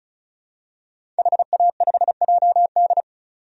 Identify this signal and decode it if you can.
Morse code, heard as a keyed tone, about 34 wpm, 700 Hz. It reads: HA5JD
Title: DXCC-100 Call Signs - 5 Characters - 35wpm